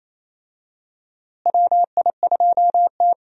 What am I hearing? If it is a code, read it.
Morse code, heard as a keyed tone, about 28 wpm, 700 Hz. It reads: WI2T
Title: USA Call Signs - 4 Characters - Repeated 2x - 28wpm